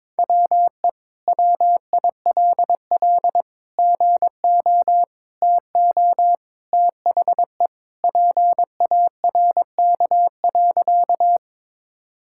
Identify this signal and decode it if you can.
Morse code, heard as a keyed tone, about 22 wpm, 700 Hz. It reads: WE WILL GO TO THE PARK.
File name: practice / 22wpm / Sentences with 2-Word Phrases - 22wpm